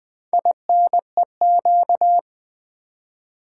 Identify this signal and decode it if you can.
Morse code, heard as a keyed tone, about 20 wpm, 700 Hz. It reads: INEQ
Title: Build A Word - 4 Letters At A Time - 10-Letter Words - 20wpm